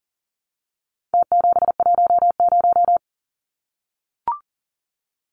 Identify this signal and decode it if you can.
Morse code, heard as a keyed tone, about 40 wpm, 700 Hz. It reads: T710
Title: Numbers - Contest Style - 4-Digit 40wpm